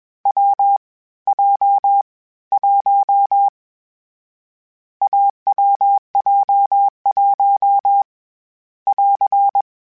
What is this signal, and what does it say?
Morse code, heard as a keyed tone, about 21 wpm, 800 Hz. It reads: W J 1 AWJ1 <AR>
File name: Difficult Combos / Difficult Combos-mp3s / AWJ1 @20